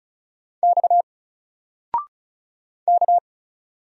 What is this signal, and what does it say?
Morse code, heard as a keyed tone, about 36 wpm, 700 Hz. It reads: X K
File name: Single-Letter-Number-35wpm